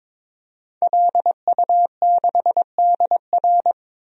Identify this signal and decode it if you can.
Morse code, heard as a keyed tone, about 22 wpm, 700 Hz. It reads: LU6DR
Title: DXCC-100 Call Signs - 5 Characters - 22wpm